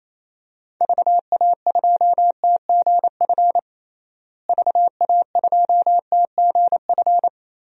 Morse code, heard as a keyed tone, about 28 wpm, 700 Hz. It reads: VA2TGF VA2TGF
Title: DXCC-100 Call Signs - 6 Characters - Repeated 3x - 28wpm